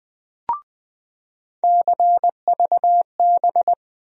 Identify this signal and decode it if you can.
Morse code, heard as a keyed tone, about 20 wpm, 700 Hz. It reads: CVB
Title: USA Call Sign Postfixes - 3 Letters 20wpm